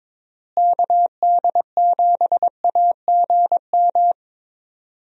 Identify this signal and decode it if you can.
Morse code, heard as a keyed tone, about 21 wpm, 700 Hz. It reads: KD7AGM
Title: USA Call Signs - 6 Characters - Repeated 2x - 22wpm